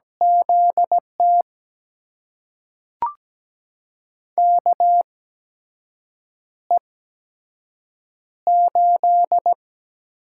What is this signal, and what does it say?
Morse code, heard as a keyed tone, about 17 wpm, spaced as for 6 wpm, 700 Hz. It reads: ZT KE8
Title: USA Call Signs - 6 Characters - 8x Character Spacing - 17wpm